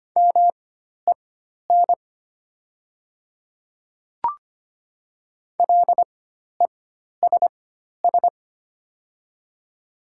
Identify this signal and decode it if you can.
Morse code, heard as a keyed tone, about 25 wpm, spaced as for 12 wpm, 700 Hz. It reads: MEN LESS
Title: 4x Character Spacing - Top 500 Words - 25wpm